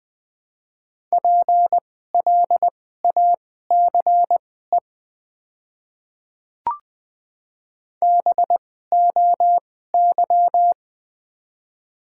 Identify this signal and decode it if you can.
Morse code, heard as a keyed tone, about 21 wpm, spaced as for 15 wpm, 700 Hz. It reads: PLACE BOY
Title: 2x Character Spacing - Top 200 Words - 20wpm